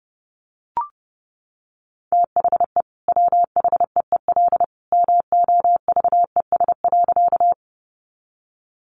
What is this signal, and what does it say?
Morse code, heard as a keyed tone, about 30 wpm, 700 Hz. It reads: THE WHEEL MOVES.
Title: Additional Easy Phrases 30wpm